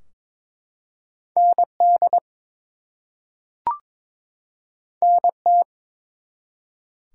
Morse code, heard as a keyed tone, about 22 wpm, 700 Hz. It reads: ND NT
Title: US States and Canadian Provinces 22wpm